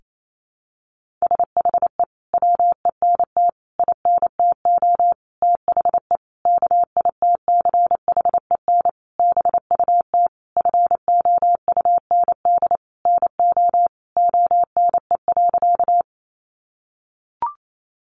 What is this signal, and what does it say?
Morse code, heard as a keyed tone, about 28 wpm, 700 Hz. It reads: SHE WENT INTO THE KITCHEN BUT FOUND NO ONE.
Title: Sentences from Top 1400 Words 28wpm